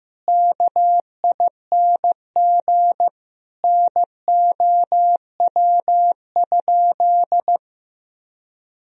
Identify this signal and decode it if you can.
Morse code, heard as a keyed tone, about 16 wpm, 700 Hz. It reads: KING NOW?
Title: Sentences from Top 1000 Words - Review Part 4 - 15wpm